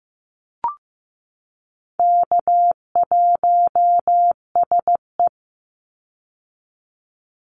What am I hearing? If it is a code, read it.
Morse code, heard as a keyed tone, about 15 wpm, 700 Hz. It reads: K1SE
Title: Easy Call Signs Repeated 2x - 15wpm